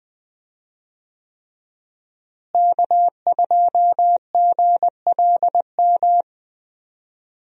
Morse code, heard as a keyed tone, about 20 wpm, 700 Hz. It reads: K2GLM
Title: USA Call Signs - 5 Characters - Repeated 3x - 20wpm